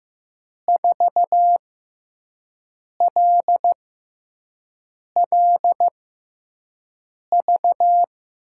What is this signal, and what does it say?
Morse code, heard as a keyed tone, about 15 wpm, spaced as for 5 wpm, 700 Hz. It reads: 4LLV
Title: USA Call Signs - 6 Characters - 6x Character Spacing - 15wpm